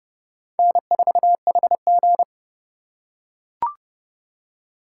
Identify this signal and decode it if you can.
Morse code, heard as a keyed tone, about 30 wpm, 700 Hz. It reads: N4HG
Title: USA Call Signs - 4 Characters - 3x Character Spacing - 30wpm